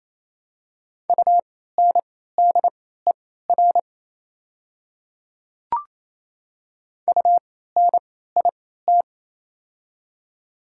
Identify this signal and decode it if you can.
Morse code, heard as a keyed tone, about 27 wpm, spaced as for 16 wpm, 700 Hz. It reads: UNDER UNIT
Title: 3x Character Spacing - Top 500 Words - 28wpm